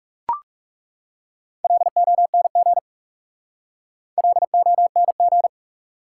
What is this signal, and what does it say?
Morse code, heard as a keyed tone, about 40 wpm, 700 Hz. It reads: LONG LONG